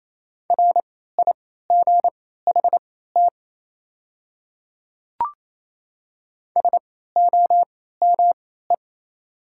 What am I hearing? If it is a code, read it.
Morse code, heard as a keyed tone, about 28 wpm, spaced as for 16 wpm, 700 Hz. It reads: RIGHT SOME
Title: 3x Character Spacing - Top 300 Words - 28wpm